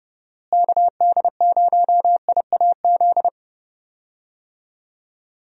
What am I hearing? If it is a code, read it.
Morse code, heard as a keyed tone, about 32 wpm, 700 Hz. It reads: KD0IAZ
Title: USA Call Signs - 6 Characters - Repeated 2x - 30wpm